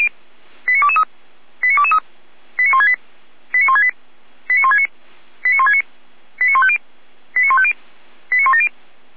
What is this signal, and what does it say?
Five-tone selective call (ZVEI), heard as quick runs of five tones, tones of about 70 ms. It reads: 89223 89223 89168 89168 89169 89169 89150 89150 89170
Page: audio.f85360.feuerwehr-orlamuende.org